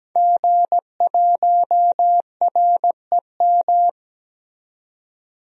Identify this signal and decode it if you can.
Morse code, heard as a keyed tone, about 17 wpm, 700 Hz. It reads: G1REM
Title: DXCC-100 Call Signs - 5 Characters - 17wpm